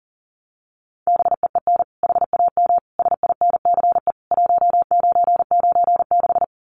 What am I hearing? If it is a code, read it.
Morse code, heard as a keyed tone, about 40 wpm, 700 Hz. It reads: BEEN HAM SINCE 1996
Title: Common QSO Phrases 40wpm